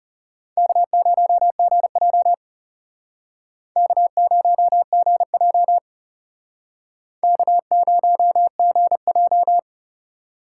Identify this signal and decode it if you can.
Morse code, heard as a keyed tone, about 37 wpm, 700 Hz. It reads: K0GJ K0GJ K0GJ
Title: Speed Racing - USA 1x2 Call Signs 30wpm